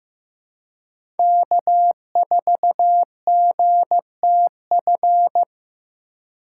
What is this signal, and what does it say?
Morse code, heard as a keyed tone, about 15 wpm, 700 Hz. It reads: K4GTF
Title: USA Call Signs - 5 Characters - 3x Character Spacing - 15wpm